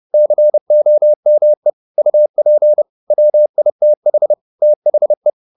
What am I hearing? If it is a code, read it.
Morse code, heard as a keyed tone, about 30 wpm, 600 Hz. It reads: COME UP WITH THE
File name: the old man and the sea / the old man and the sea 1 30wpm